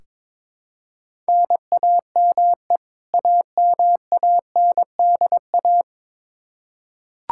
Morse code, heard as a keyed tone, about 22 wpm, 700 Hz. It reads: NAME AMANDA
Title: Practice QSO - Exchange 1 - Beginning - 22wpm